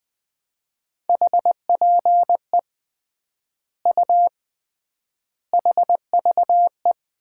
Morse code, heard as a keyed tone, about 20 wpm, 700 Hz. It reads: HPE U HVE
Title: Common QSO Phrases - 3x Word-Spacing - 20wpm